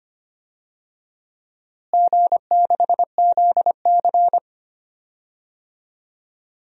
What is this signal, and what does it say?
Morse code, heard as a keyed tone, about 25 wpm, 700 Hz. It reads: G6ZC
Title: DXCC-100 Call Signs - 4 Characters - Repeated 2x - 25wpm